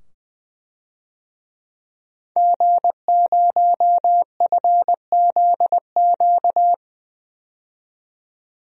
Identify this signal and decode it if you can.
Morse code, heard as a keyed tone, about 20 wpm, 700 Hz. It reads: G0FZQ
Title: DXCC-100 Call Signs - 5 Characters - 20wpm